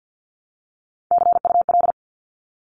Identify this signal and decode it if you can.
Morse code, heard as a keyed tone, about 50 wpm, 700 Hz. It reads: CUL
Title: Abbreviations 50wpm